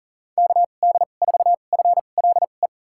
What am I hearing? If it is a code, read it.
Morse code, heard as a keyed tone, about 39 wpm, spaced as for 28 wpm, 700 Hz. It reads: KD4FLE
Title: USA Call Signs - 6 Characters - 2x Character Spacing - 40wpm